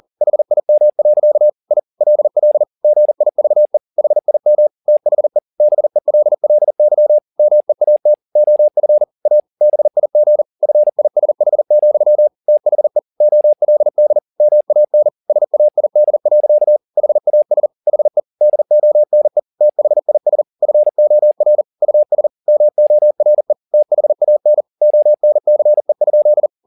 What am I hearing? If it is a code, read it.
Morse code, heard as a keyed tone, about 40 wpm, 600 Hz. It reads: HIM. I LL GIVE HIM THE BELLY MEAT OF A BIG FISH, THE OLD MAN SAID. HAS HE DONE THIS FOR US MORE THAN ONCE?